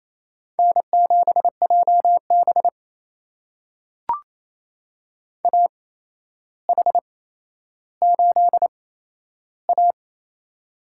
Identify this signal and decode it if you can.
Morse code, heard as a keyed tone, about 28 wpm, spaced as for 12 wpm, 700 Hz. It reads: N7JB AH8A
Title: USA Call Signs - 4 Characters - 8x Character Spacing - 28wpm